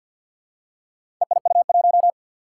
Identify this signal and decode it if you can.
Morse code, heard as a keyed tone, about 50 wpm, 700 Hz. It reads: EIU1
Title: Four Challenging Letter-Number Combinations 50wpm